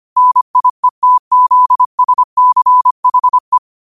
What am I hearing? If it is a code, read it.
Morse code, heard as a keyed tone, about 25 wpm, 1000 Hz. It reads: NIETZSCHE